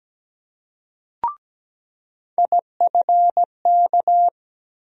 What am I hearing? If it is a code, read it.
Morse code, heard as a keyed tone, about 17 wpm, 700 Hz. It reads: IFK
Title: USA Call Sign Postfixes - 3 Letters 17wpm